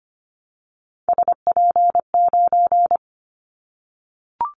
Morse code, heard as a keyed tone, about 25 wpm, 700 Hz. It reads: SP9
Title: Callsign Prefixes for Top 25 DXCC - 25wpm